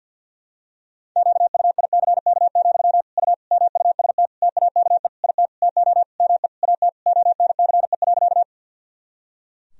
Morse code, heard as a keyed tone, about 46 wpm, 700 Hz. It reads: QUICK, U MUST TAKE IT TO ME AT ONCE.
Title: Sentences from Top 500 Words and Abbreviations 50wpm